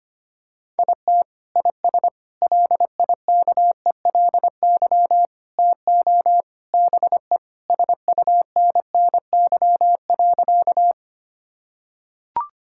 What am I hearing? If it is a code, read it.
Morse code, heard as a keyed tone, about 25 wpm, 700 Hz. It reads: IT IS LIKELY TO BE SUNNY.
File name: practice / 25wpm / Sentences with 3-Word Phrases - 25wpm